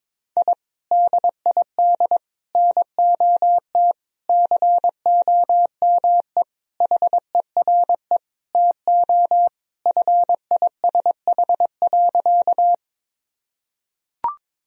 Morse code, heard as a keyed tone, about 22 wpm, 700 Hz. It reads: I DID NOT COME HERE TO FISH.
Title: Sentences from Top 1000 Words - Review Part 1 - 22wpm